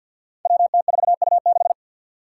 Morse code, heard as a keyed tone, about 52 wpm, 700 Hz. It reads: WT4U6
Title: Five Challenging Letter-Number Combinations 50wpm